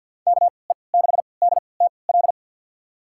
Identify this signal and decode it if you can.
Morse code, heard as a keyed tone, about 48 wpm, spaced as for 28 wpm, 700 Hz. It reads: KE6DTL
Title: USA Call Signs - 6 Characters - 3x Character Spacing - 50wpm